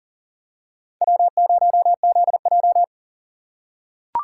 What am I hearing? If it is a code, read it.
Morse code, heard as a keyed tone, about 39 wpm, 700 Hz. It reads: W0ZJ
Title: USA Call Signs - 4 Characters - 4x Character Spacing - 40wpm